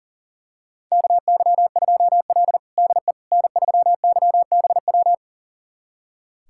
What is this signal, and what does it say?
Morse code, heard as a keyed tone, about 40 wpm, 700 Hz. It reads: KY2L DE N3YBW
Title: USA Callsign DE Callsign 40wpm